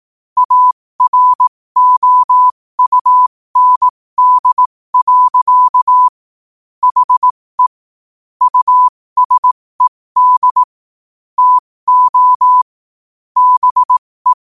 Morse code, heard as a keyed tone, about 18 wpm, spaced as for 15 wpm, 1000 Hz. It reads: AROUND. HE USED TO BE
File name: qotdfiles / 15wpm